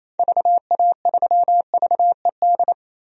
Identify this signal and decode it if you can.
Morse code, heard as a keyed tone, about 28 wpm, 700 Hz. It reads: VA3VED
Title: DXCC-100 Call Signs - 6 Characters - 28wpm